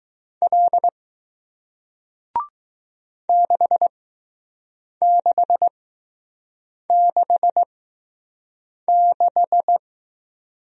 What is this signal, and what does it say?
Morse code, heard as a keyed tone, about 20 wpm, spaced as for 6 wpm, 700 Hz. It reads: L 6666